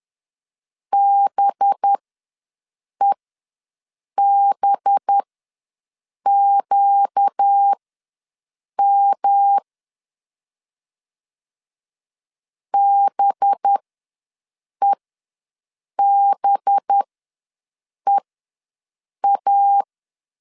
Morse code, heard as a keyed tone, about 10 wpm, spaced as for 6 wpm, 800 Hz. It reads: BEBQM BEBEA
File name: LEZIONI / 5 LBVC / 5 LB ws 10wpm 3616